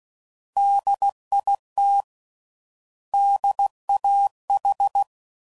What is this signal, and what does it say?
Morse code, heard as a keyed tone, about 16 wpm, 800 Hz. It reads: DIT DAH
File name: Prefixes / By First Letter / Ds